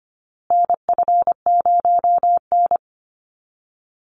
Morse code, heard as a keyed tone, about 25 wpm, 700 Hz. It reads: NF0N
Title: Call Signs Repeated 3x - 25wpm